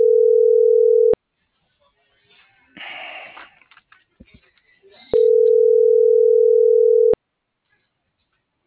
Ringback tone (North America), 440 + 480 Hz.